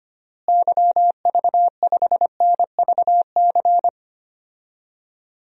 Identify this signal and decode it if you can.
Morse code, heard as a keyed tone, about 25 wpm, 700 Hz. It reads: YV5NVC
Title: DXCC-100 Call Signs - 6 Characters - Repeated 3x - 25wpm